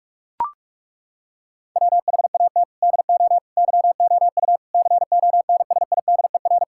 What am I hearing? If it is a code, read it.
Morse code, heard as a keyed tone, about 41 wpm, 700 Hz. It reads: WHAT DO YOU CONSIDER
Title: Sentences from Top 1000 Words - Review Part 5 - 45wpm